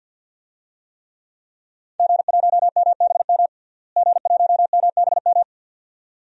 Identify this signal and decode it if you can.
Morse code, heard as a keyed tone, about 46 wpm, 700 Hz. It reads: G1MBM G1MBM
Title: DXCC-100 Call Signs - 5 Characters - Repeated 2x - 50wpm